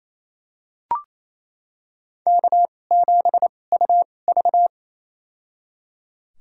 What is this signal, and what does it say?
Morse code, heard as a keyed tone, about 28 wpm, spaced as for 20 wpm, 700 Hz. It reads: K7UV
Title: USA Call Signs - 2x Character Spacing - 28wpm